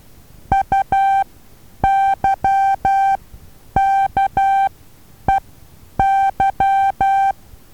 Morse code, heard as a keyed tone, about 12 wpm, spaced as for 9 wpm, 780 Hz. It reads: UYKEY
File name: Taviro HA6VA B